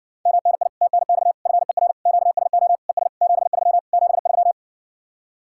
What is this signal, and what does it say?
Morse code, heard as a keyed tone, about 70 wpm, 700 Hz. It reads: MNI TNX FER QSO ES 73 73